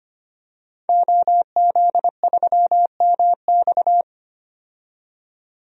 Morse code, heard as a keyed tone, about 25 wpm, 700 Hz. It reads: OZ3MX
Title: DXCC-100 Call Signs - 5 Characters - 25wpm